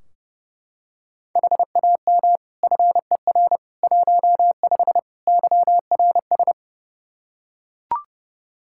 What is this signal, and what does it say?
Morse code, heard as a keyed tone, about 30 wpm, 700 Hz. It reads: HAM FER 15 YRS
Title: Practice QSO - Exchange 3 - More Info - 30wpm